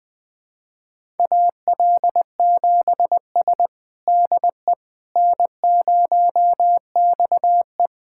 Morse code, heard as a keyed tone, about 20 wpm, 700 Hz. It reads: AL7S DE N0XE
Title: USA Callsign DE Callsign - 4 Characters - 20wpm